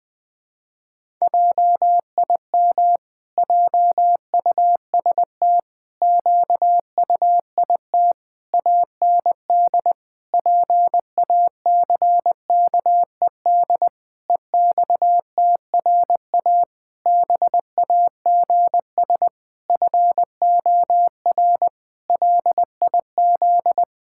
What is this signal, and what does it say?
Morse code, heard as a keyed tone, about 21 wpm, 700 Hz. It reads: JIM JUST QUIT AND PACKED EXTRA BAGS FOR LIZ
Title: Pangram Challenge v2 - 20wpm